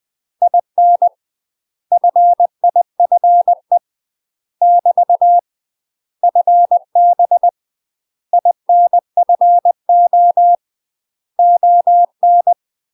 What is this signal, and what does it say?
Morse code, heard as a keyed tone, about 20 wpm, 700 Hz. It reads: IN FIFE <BT> FB INFO ON
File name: UK MASTER QSOs TRAINING / 20WPM / UK MASTER QSOs 0020